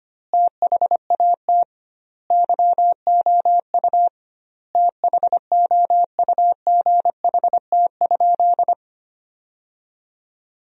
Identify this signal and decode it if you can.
Morse code, heard as a keyed tone, about 25 wpm, 700 Hz. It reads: THAT YOU THOUGHT?
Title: 2x Word-Spacing - Sentences from Top 300 Words - 25wpm